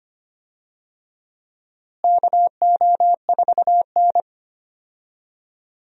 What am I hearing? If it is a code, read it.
Morse code, heard as a keyed tone, about 25 wpm, 700 Hz. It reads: KO4N
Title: USA Call Signs - 4 Characters - Repeated 3x - 25wpm